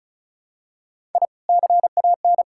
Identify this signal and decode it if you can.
Morse code, heard as a keyed tone, about 34 wpm, 700 Hz. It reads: I CAN
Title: Sentences from Top 1000 Words - Review Part 6 - 35wpm